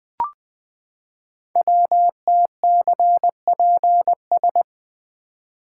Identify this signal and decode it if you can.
Morse code, heard as a keyed tone, about 20 wpm, 700 Hz. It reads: WTCPS
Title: Code Groups - 5 Letters - 20wpm